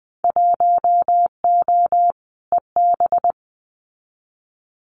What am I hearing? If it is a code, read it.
Morse code, heard as a keyed tone, about 20 wpm, 700 Hz. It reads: 1O EB
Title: Winter Field Day Exchanges 20wpm